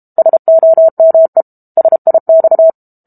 Morse code, heard as a keyed tone, about 32 wpm, 650 Hz. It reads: SOME SIX
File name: Book Two / 06-Chapter 6 - 32 WPM